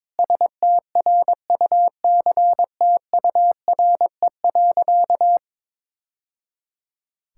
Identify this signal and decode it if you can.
Morse code, heard as a keyed tone, about 22 wpm, 700 Hz. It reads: STRUCTURE.
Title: Sentences from Top 1000 Words - Review Part 6 - 22wpm